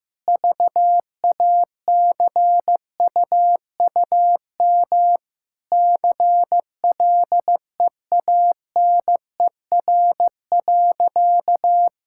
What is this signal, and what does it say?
Morse code, heard as a keyed tone, about 15 wpm, 700 Hz. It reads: VACUUM CLEANER.
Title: Pangram Challenge v2 - 15wpm